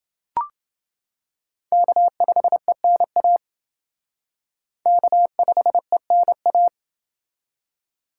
Morse code, heard as a keyed tone, about 28 wpm, 700 Hz. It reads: K5ENA K5ENA